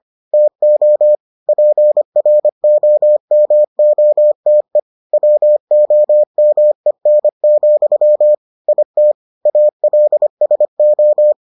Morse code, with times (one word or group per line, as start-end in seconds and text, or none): 0.3-1.1 TO
1.5-4.8 PROMOTE
5.1-8.3 WOMEN,
8.7-9.1 IT
9.5-11.3 ALSO